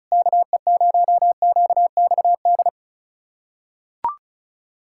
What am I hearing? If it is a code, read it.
Morse code, heard as a keyed tone, about 36 wpm, 700 Hz. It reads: KE0QXD
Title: USA Callsign DE Callsign - 6 Characters - 35wpm